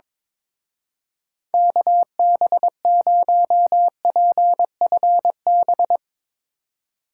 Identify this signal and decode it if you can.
Morse code, heard as a keyed tone, about 22 wpm, 700 Hz. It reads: KB0PFB
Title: Speed Racing - USA Call Signs 22wpm